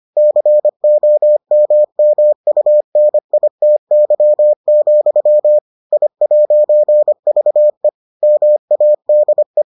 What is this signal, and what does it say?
Morse code, heard as a keyed tone, about 25 wpm, 600 Hz. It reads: COMMUNITY, I'VE MADE